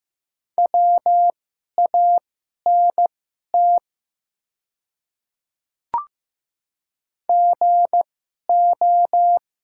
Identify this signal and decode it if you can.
Morse code, heard as a keyed tone, about 15 wpm, spaced as for 11 wpm, 700 Hz. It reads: WANT GO
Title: 2x Character Spacing - Top 200 Words - 15wpm